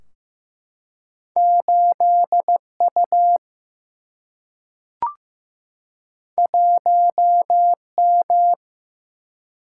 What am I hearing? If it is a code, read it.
Morse code, heard as a keyed tone, about 15 wpm, 700 Hz. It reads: 8U 1M
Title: Single-Letter Single-Number 15wpm